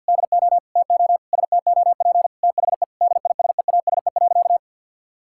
Morse code, heard as a keyed tone, about 47 wpm, 700 Hz. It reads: DO TO STOP THE DISEASE.
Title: Sentences from Top 1000 Words - Review Part 6 - 50wpm